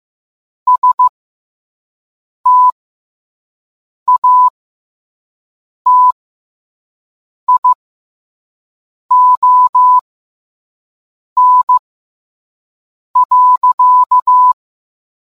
Morse code, heard as a keyed tone, about 15 wpm, spaced as for 5 wpm, 1000 Hz. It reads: STATION.